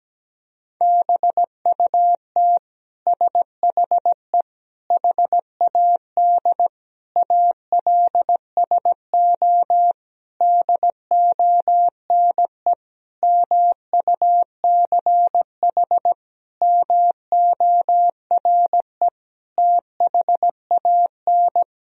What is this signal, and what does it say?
Morse code, heard as a keyed tone, about 17 wpm, 700 Hz. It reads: BUT SHE HAD ALSO DONE MUCH MORE THAN
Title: Sentences from Top 1000 Words - Review Part 1 - 17wpm